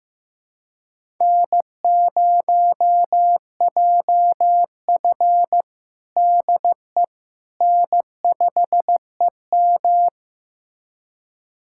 Morse code, heard as a keyed tone, about 16 wpm, 700 Hz. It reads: N0JF DE N5EM
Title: USA Callsign DE Callsign - 4 Characters - 15wpm